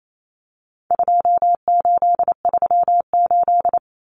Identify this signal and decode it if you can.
Morse code, heard as a keyed tone, about 28 wpm, 700 Hz. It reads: 2838
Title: Common Cut Numbers - 4-Digit 28wpm